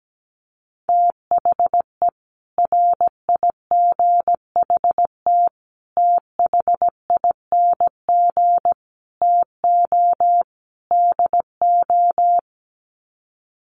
Morse code, heard as a keyed tone, about 17 wpm, 700 Hz. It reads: THE RIGHT THING TO DO
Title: Sets of 5 Words 17wpm